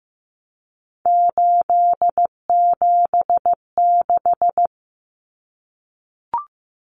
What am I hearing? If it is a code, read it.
Morse code, heard as a keyed tone, about 15 wpm, 700 Hz. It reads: 876
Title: Numbers - 3-Digit 15wpm